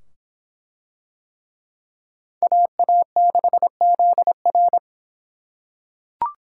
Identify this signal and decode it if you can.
Morse code, heard as a keyed tone, about 22 wpm, 700 Hz. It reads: AA6ZR